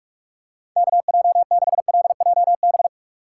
Morse code, heard as a keyed tone, about 46 wpm, 700 Hz. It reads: KJ6LJB
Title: USA Call Signs - 6 Characters - 3x Character Spacing - 45wpm